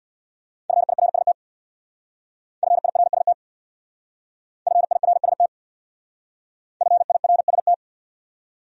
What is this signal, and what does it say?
Morse code, heard as a keyed tone, about 66 wpm, 700 Hz. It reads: FIRST FIRST FIRST FIRST